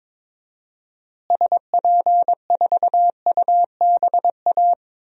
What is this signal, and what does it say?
Morse code, heard as a keyed tone, about 22 wpm, 700 Hz. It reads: SP4UBA
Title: DXCC-100 Call Signs - 6 Characters - Repeated 2x - 22wpm